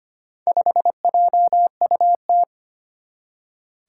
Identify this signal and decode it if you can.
Morse code, heard as a keyed tone, about 25 wpm, 700 Hz. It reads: 5JUT
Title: Four Challenging Letter-Number Combinations 25wpm